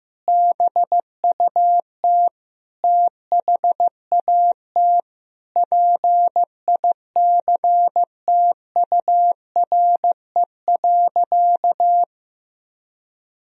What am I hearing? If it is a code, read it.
Morse code, heard as a keyed tone, about 15 wpm, 700 Hz. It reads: BUT THAT PICTURE.
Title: Sentences from Top 1000 Words - Review Part 1 - 15wpm